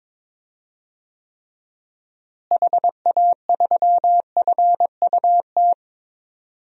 Morse code, heard as a keyed tone, about 22 wpm, 700 Hz. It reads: HA3FUT
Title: DXCC-100 Call Signs - 6 Characters - Repeated 2x - 22wpm